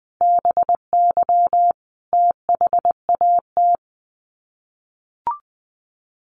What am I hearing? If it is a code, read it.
Morse code, heard as a keyed tone, about 20 wpm, 700 Hz. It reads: BY THAT
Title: Sets of 5 Words Encore2 20wpm